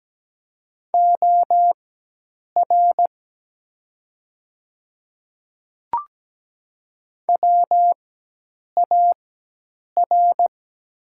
Morse code, heard as a keyed tone, about 17 wpm, spaced as for 8 wpm, 700 Hz. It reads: OR WAR
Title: 4x Character Spacing - Top 500 Words - 17wpm